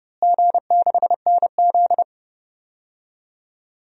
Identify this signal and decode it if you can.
Morse code, heard as a keyed tone, about 31 wpm, 700 Hz. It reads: G6NZ
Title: DXCC-100 Call Signs - 4 Characters - Repeated 2x - 30wpm